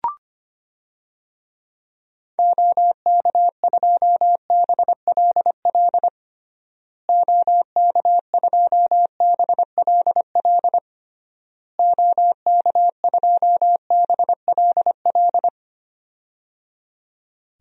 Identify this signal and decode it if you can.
Morse code, heard as a keyed tone, about 25 wpm, 700 Hz. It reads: OK2BLL OK2BLL OK2BLL
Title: DXCC-100 Call Signs - 6 Characters - Repeated 3x - 25wpm